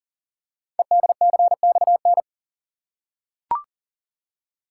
Morse code, heard as a keyed tone, about 42 wpm, 700 Hz. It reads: EDCXN
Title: Code Groups - 5 Letters - 40wpm